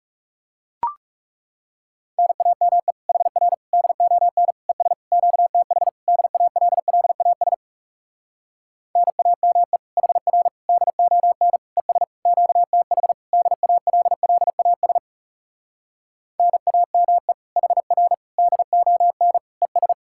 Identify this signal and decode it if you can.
Morse code, heard as a keyed tone, about 40 wpm, 700 Hz. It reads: NAME HR DON ES QTH DALLAS NAME HR DON ES QTH DALLAS NAME HR DON ES